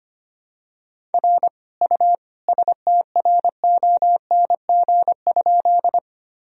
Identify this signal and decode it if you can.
Morse code, heard as a keyed tone, about 25 wpm, 700 Hz. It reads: R U STRONG?
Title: Sentences from Top 500 Words and Abbreviations 25wpm